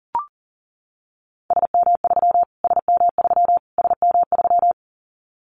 Morse code, heard as a keyed tone, about 40 wpm, 700 Hz. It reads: SM3 SM3 SM3
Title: Callsign Prefixes for Top 25 DXCC - Repeated 3x - 40wpm